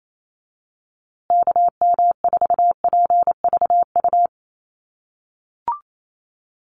Morse code, heard as a keyed tone, about 28 wpm, 700 Hz. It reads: KM4PVU
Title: USA Call Signs Repeated 3x 28wpm